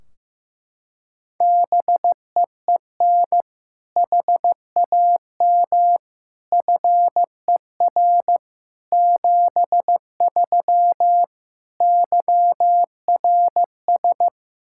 Morse code, heard as a keyed tone, about 15 wpm, 700 Hz. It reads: BEEN HAM FER 73 YRS
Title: Practice QSO - Exchange 3 - More Info - 15wpm